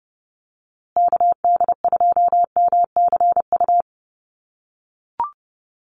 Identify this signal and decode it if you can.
Morse code, heard as a keyed tone, about 30 wpm, 700 Hz. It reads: KD2MCU
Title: USA Call Signs Repeated 2x 30wpm